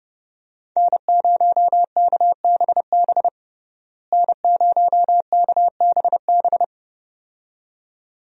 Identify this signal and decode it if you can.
Morse code, heard as a keyed tone, about 30 wpm, 700 Hz. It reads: N0KBB N0KBB
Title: USA Call Signs - 5 Characters - Repeated 2x - 30wpm